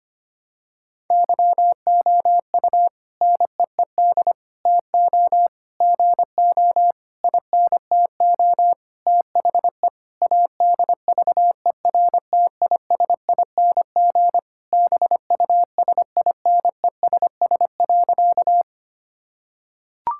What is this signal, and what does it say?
Morse code, heard as a keyed tone, about 25 wpm, 700 Hz. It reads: YOU NEED TO GO INTO THE ADVERTISING BUSINESS.
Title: Sentences from Top 1000 Words - Review Part 6 - 25wpm